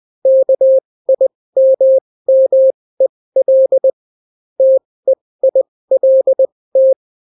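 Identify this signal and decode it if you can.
Morse code, heard as a keyed tone, about 20 wpm, spaced as for 16 wpm, 530 Hz. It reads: KIMMEL TEILT
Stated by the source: Daily Morse weather and top stories at 16 wpm, clean without QRN.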